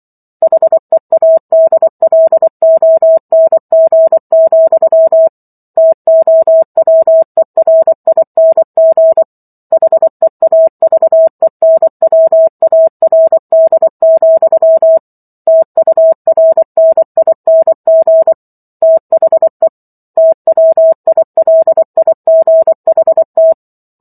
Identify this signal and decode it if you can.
Morse code, heard as a keyed tone, about 24 wpm, 650 Hz. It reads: HEADLONG, TOWERING HEAVENWARD, TURNING THE TWILIGHT